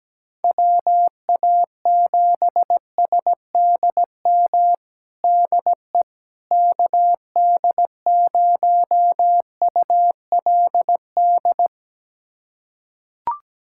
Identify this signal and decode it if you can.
Morse code, heard as a keyed tone, about 17 wpm, 700 Hz. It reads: WA7SDM DE KD0ULD
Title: USA Callsign DE Callsign - 6 Characters - 17wpm